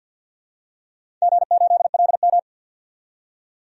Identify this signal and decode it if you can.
Morse code, heard as a keyed tone, about 49 wpm, 700 Hz. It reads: G8LM
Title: DXCC-100 Call Signs - 4 Characters - 50wpm